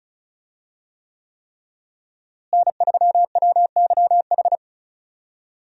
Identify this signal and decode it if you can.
Morse code, heard as a keyed tone, about 34 wpm, 700 Hz. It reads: N3WYH
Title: USA Call Signs - 5 Characters - Repeated 3x - 35wpm